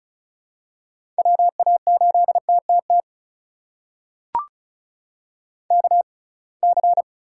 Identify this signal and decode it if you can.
Morse code, heard as a keyed tone, about 36 wpm, spaced as for 27 wpm, 700 Hz. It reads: WA8TTT KC
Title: USA Call Signs - 6 Characters - 6x Character Spacing - 35wpm